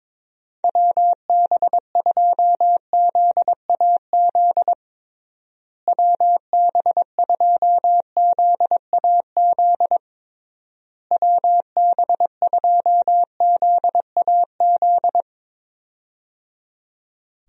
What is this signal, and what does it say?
Morse code, heard as a keyed tone, about 22 wpm, 700 Hz. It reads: WB2ZAZ WB2ZAZ WB2ZAZ
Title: USA Call Signs - 6 Characters - Repeated 3x - 22wpm